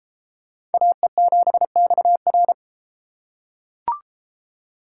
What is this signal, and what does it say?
Morse code, heard as a keyed tone, about 31 wpm, 700 Hz. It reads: AE7XR